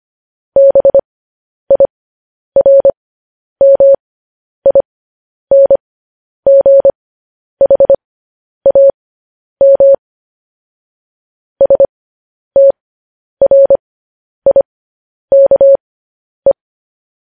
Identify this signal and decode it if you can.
Morse code, heard as a keyed tone, about 25 wpm, spaced as for 10 wpm, 550 Hz. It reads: BIRMINGHAM STRIKE
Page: news headlines in morse code at 10 wpm